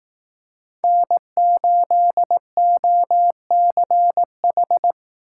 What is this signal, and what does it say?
Morse code, heard as a keyed tone, about 18 wpm, 700 Hz. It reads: N8OCH
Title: Speed Racing - USA 1x3 Call Signs 15wpm